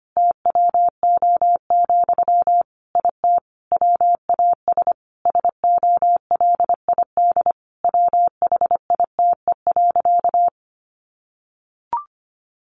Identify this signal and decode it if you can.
Morse code, heard as a keyed tone, about 25 wpm, 700 Hz. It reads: TWO, IT WAS SOLID WHITE.
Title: Sentences from Top 1500 Words 25wpm